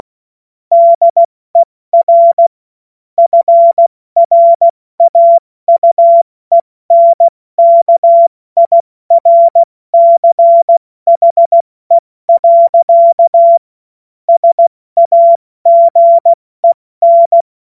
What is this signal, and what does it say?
Morse code, heard as a keyed tone, about 16 wpm, 680 Hz. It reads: DER FRAUENKIRCHE. SAGEN
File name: Morsekurs / Witze / Witz 06. Januar 2026 70 F80